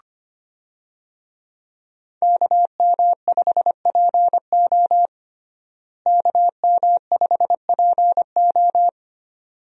Morse code, heard as a keyed tone, about 25 wpm, 700 Hz. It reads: KM5PO KM5PO
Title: USA Call Signs - 5 Characters - Repeated 2x - 25wpm